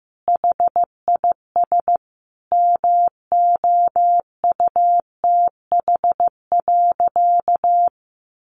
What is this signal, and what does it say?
Morse code, heard as a keyed tone, about 15 wpm, 700 Hz. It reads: HIS MOUTH.
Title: Sentences from Top 1500 Words 15wpm